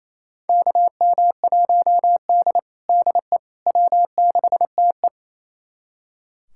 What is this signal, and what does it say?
Morse code, heard as a keyed tone, about 28 wpm, 700 Hz. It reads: KM1D DE W6TE
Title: USA Callsign DE Callsign - 4 Characters - 28wpm